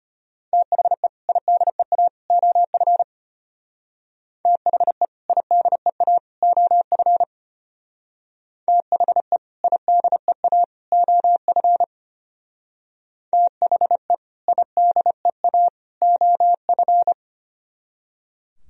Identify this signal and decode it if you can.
Morse code, heard as a keyed tone, about 32 wpm, 700 Hz. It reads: THE IDEA OF THE IDEA OF THE IDEA OF THE IDEA OF